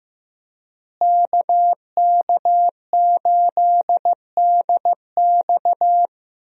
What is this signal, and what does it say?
Morse code, heard as a keyed tone, about 15 wpm, 700 Hz. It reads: KK8DX
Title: USA Call Signs - 5 Characters - Repeated 2x - 15wpm